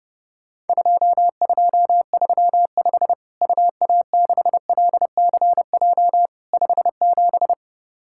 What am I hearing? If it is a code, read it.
Morse code, heard as a keyed tone, about 30 wpm, 700 Hz. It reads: 2235 UA6LCJ 57
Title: Worked All Europe DX Contest - QTC Exchange - 30wpm